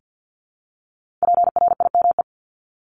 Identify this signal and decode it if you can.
Morse code, heard as a keyed tone, about 50 wpm, 700 Hz. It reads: PRIME
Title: Multi-letter Phonemes Repeated 3x 50wpm